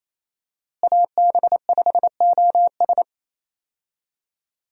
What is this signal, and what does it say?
Morse code, heard as a keyed tone, about 28 wpm, 700 Hz. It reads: AB5OS
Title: USA Call Signs - 5 Characters - Repeated 2x - 28wpm